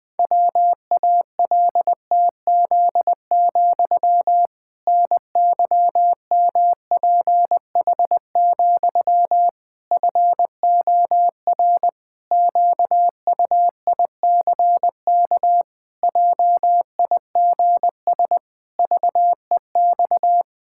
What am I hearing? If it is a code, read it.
Morse code, heard as a keyed tone, about 20 wpm, 700 Hz. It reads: WALTZ, NYMPH, FOR QUICK JIGS VEX
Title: Pangram Challenge v2 - 20wpm